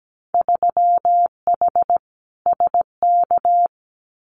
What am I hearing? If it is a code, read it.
Morse code, heard as a keyed tone, about 17 wpm, 700 Hz. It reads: 3H SK
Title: Winter Field Day Exchanges 17wpm